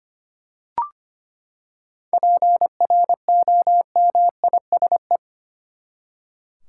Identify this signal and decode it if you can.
Morse code, heard as a keyed tone, about 25 wpm, 700 Hz. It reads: PROMISE
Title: Seven Letter Words - 25wpm